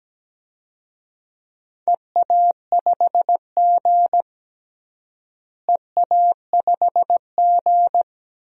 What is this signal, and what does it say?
Morse code, heard as a keyed tone, about 17 wpm, 700 Hz. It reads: EA5G EA5G
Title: DXCC-100 Call Signs - 4 Characters - Repeated 3x - 17wpm